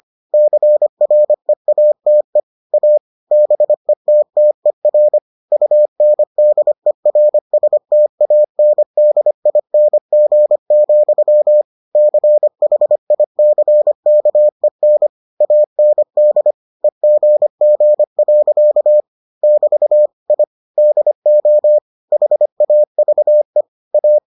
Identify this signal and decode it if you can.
Morse code, heard as a keyed tone, about 25 wpm, 600 Hz. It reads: CREATE A BETTER UNDERSTANDING, CHICKEN AND EGG. <BT> I DO HAVE A